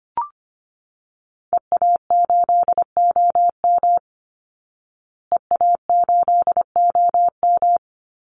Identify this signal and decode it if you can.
Morse code, heard as a keyed tone, about 25 wpm, 700 Hz. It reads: EA8OM EA8OM
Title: Call Signs Repeated 3x - 25wpm